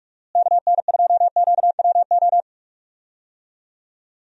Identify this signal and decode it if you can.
Morse code, heard as a keyed tone, about 46 wpm, 700 Hz. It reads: KN2QWO
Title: USA Call Signs - 6 Characters - Repeated 2x - 45wpm